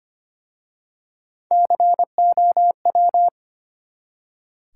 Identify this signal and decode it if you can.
Morse code, heard as a keyed tone, about 25 wpm, 700 Hz. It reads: COW
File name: practice / 25wpm / Easy Words with Hints - 25wpm